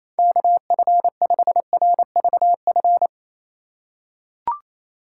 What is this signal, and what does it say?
Morse code, heard as a keyed tone, about 28 wpm, 700 Hz. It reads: KF5RVF
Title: USA Call Signs - 6 Characters - 4x Character Spacing - 28wpm